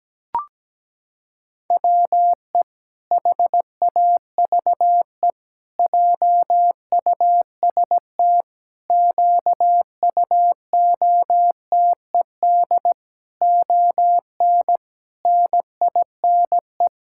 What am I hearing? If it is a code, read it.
Morse code, heard as a keyed tone, about 17 wpm, 700 Hz. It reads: WE HAVE JUST QUOTED ON NINE
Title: Pangram Challenge v2 - 17wpm